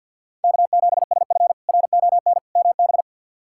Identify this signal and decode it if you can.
Morse code, heard as a keyed tone, about 51 wpm, 700 Hz. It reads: K7SF RON MB
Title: K1USN SST QSO Practice - 50wpm